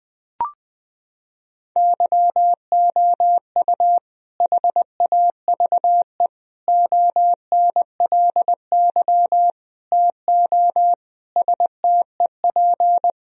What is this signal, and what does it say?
Morse code, heard as a keyed tone, about 20 wpm, 700 Hz. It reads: YOU HAVE ONLY TO STEP
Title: Sentences from Top 1000 Words - Review Part 4 - 20wpm